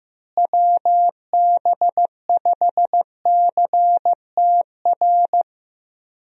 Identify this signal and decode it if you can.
Morse code, heard as a keyed tone, about 15 wpm, 700 Hz. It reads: WB5CTR
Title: USA Call Signs - 6 Characters - Repeated 2x - 15wpm